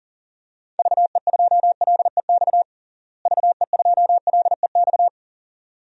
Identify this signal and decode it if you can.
Morse code, heard as a keyed tone, about 40 wpm, 700 Hz. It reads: VE2LEX VE2LEX
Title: DXCC-100 Call Signs - 6 Characters - Repeated 2x - 40wpm